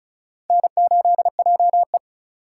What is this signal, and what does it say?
Morse code, heard as a keyed tone, about 46 wpm, 700 Hz. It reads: N8JE